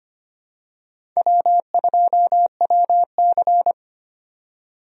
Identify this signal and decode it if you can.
Morse code, heard as a keyed tone, about 25 wpm, 700 Hz. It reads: W2WC
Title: USA Call Signs - 4 Characters - 25wpm